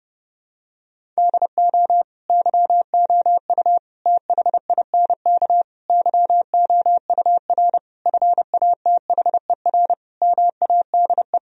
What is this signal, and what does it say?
Morse code, heard as a keyed tone, about 30 wpm, 700 Hz. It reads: DO YOU THINK YOUR FATHER MADE
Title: Sentences from Top 1000 Words - Review Part 1 - 30wpm